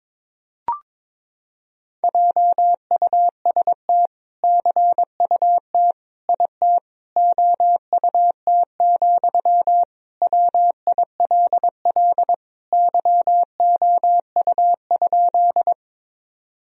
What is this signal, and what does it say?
Morse code, heard as a keyed tone, about 22 wpm, 700 Hz. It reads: JUST CUT IT OUT, WILL YOU?
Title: Sentences from Top 1000 Words - Review Part 1 - 22wpm